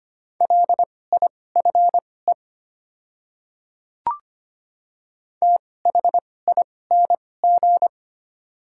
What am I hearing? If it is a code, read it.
Morse code, heard as a keyed tone, about 25 wpm, spaced as for 18 wpm, 700 Hz. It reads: LIFE THING
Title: 2x Character Spacing - Top 500 Words - 25wpm